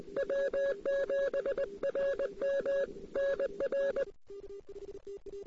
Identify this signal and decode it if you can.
Morse code, heard as a keyed tone, about 20 wpm, 550 Hz, with noise. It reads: W7RM NR
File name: CW / 2008-ss-cw-k5tr-op-n5rz -00:00